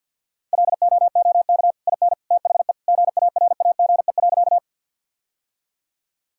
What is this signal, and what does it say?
Morse code, heard as a keyed tone, about 50 wpm, 700 Hz. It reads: LOOK IN THE GARAGE.
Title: Sentences from Top 1600 Words 50wpm